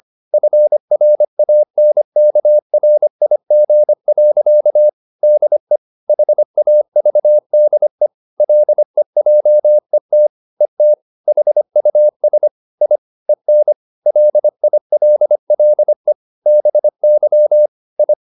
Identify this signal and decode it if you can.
Morse code, heard as a keyed tone, about 25 wpm, 600 Hz. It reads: FRANKRIG. DE HAVDE LEJET ET HUS I EN LILLE BY I
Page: Ferie 25 wpm